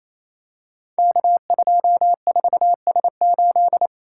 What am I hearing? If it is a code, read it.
Morse code, heard as a keyed tone, about 29 wpm, 700 Hz. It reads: K24S8
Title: Code Groups - 5 Letter:Numbers - 28wpm